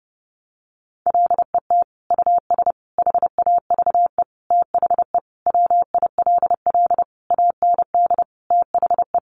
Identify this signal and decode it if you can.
Morse code, heard as a keyed tone, about 28 wpm, 700 Hz. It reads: LET US HAVE THE WILL AND THE
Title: Sentences from Top 1300 Words 30wpm